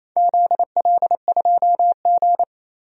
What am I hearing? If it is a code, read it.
Morse code, heard as a keyed tone, about 27 wpm, 700 Hz. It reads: ZL2G
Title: DXCC-100 Call Signs - 4 Characters - 28wpm